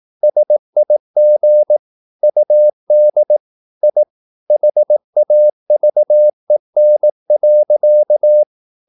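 Morse code, heard as a keyed tone, about 18 wpm, 600 Hz. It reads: SIG UD I HAVEN.
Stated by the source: Slow (QRS) Morse practice at 18 wpm, CW med Gnister.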